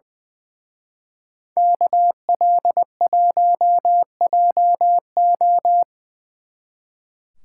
Morse code, heard as a keyed tone, about 24 wpm, 700 Hz. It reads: KL1JO
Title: Speed Racing - USA 2x2 Call Signs 20wpm